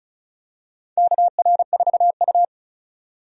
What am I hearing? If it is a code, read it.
Morse code, heard as a keyed tone, about 35 wpm, 700 Hz. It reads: KR4U
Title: USA Call Signs - 4 Characters - 35wpm